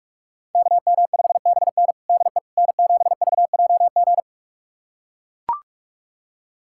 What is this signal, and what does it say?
Morse code, heard as a keyed tone, about 42 wpm, 700 Hz. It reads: KM5BN DE N7VJG
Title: USA Callsign DE Callsign - 5 Characters - 45wpm